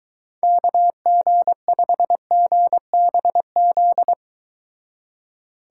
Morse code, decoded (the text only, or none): KG5GBZ